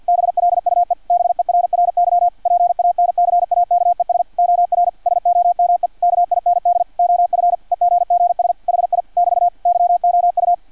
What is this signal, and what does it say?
Morse code, heard as a keyed tone, about 50 wpm, 700 Hz, with radio noise. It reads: BLUE BERRY PANCAKES OR SOME KIND OF EGGS HI <BT> YOU
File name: archived / CRSnet